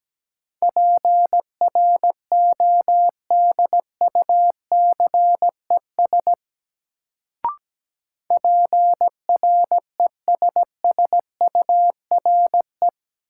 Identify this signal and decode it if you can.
Morse code, heard as a keyed tone, about 17 wpm, 700 Hz. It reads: PRODUCES PRESSURE
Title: Sentences from Top 1000 Words - Review Part 6 - New Word Drill - 17wpm